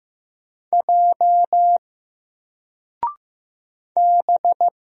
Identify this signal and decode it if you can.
Morse code, heard as a keyed tone, about 15 wpm, 700 Hz. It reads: J B
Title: Single Character (Letter:Number:Punctuation:Prosign) 15wpm